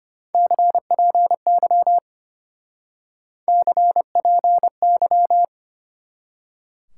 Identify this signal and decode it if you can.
Morse code, heard as a keyed tone, about 33 wpm, 700 Hz. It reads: CPY CPY